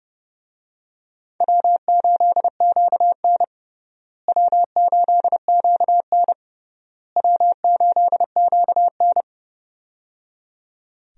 Morse code, heard as a keyed tone, about 30 wpm, 700 Hz. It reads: W8QN W8QN W8QN
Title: USA Call Signs - 4 Characters - Repeated 3x - 30wpm